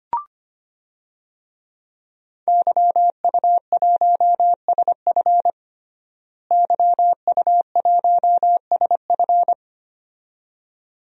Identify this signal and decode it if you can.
Morse code, heard as a keyed tone, about 25 wpm, 700 Hz. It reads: YU1SF YU1SF
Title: DXCC-100 Call Signs - 5 Characters - Repeated 2x - 25wpm